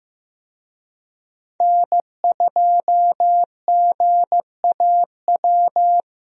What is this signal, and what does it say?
Morse code, heard as a keyed tone, about 14 wpm, 700 Hz. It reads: N2GAW
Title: USA Call Signs - 5 Characters - Repeated 3x - 15wpm